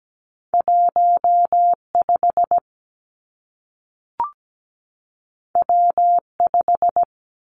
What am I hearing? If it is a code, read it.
Morse code, heard as a keyed tone, about 17 wpm, 700 Hz. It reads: 15 W5
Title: Two Challenging Letter-Number Combinations 17wpm